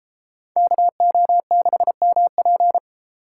Morse code, heard as a keyed tone, about 31 wpm, 700 Hz. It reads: KO6MP